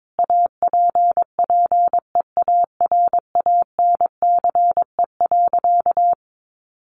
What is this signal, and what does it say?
Morse code, heard as a keyed tone, about 22 wpm, 700 Hz. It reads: APPEARANCE.
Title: Sentences from Top 1100 Words 22wpm